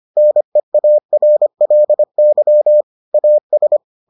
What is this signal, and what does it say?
Morse code, heard as a keyed tone, about 25 wpm, 600 Hz. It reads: NEARLY AS